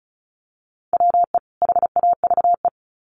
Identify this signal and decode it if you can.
Morse code, heard as a keyed tone, about 34 wpm, 700 Hz. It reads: WE HAVE
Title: Sets of 2 Words 35wpm